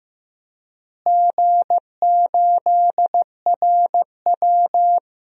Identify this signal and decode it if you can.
Morse code, heard as a keyed tone, about 15 wpm, 700 Hz. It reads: G8RW
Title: DXCC-100 Call Signs - 4 Characters - Repeated 3x - 15wpm